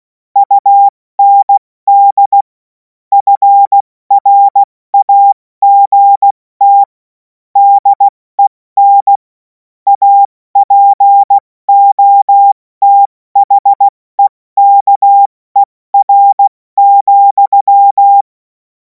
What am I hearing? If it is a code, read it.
Morse code, heard as a keyed tone, about 16 wpm, 800 Hz. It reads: UND FRAGT DEN APOTHEKER,
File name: Morsekurs / Witze / Witz 17. März 2026 70 F80